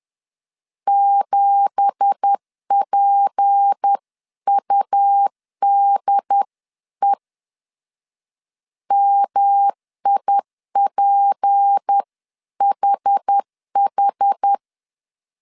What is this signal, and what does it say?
Morse code, heard as a keyed tone, about 10 wpm, 800 Hz. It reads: 7PUDE MIPHH
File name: CW / DET / 1 9 1500C S B 10wpm 4439